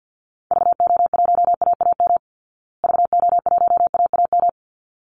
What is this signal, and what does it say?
Morse code, heard as a keyed tone, about 50 wpm, 700 Hz. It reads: VO1AAM VO1AAM
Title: Call Signs Repeated 3x - 50wpm